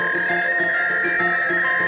Fax machine handshake.